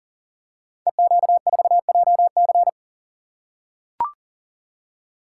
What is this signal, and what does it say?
Morse code, heard as a keyed tone, about 42 wpm, 700 Hz. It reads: EQ4JC
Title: Code Groups - 5 Letter:Numbers - 40wpm